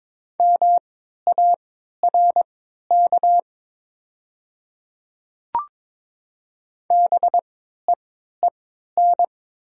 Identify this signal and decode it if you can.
Morse code, heard as a keyed tone, about 22 wpm, spaced as for 12 wpm, 700 Hz. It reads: MARK BEEN
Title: 3x Character Spacing - Top 300 Words - 22wpm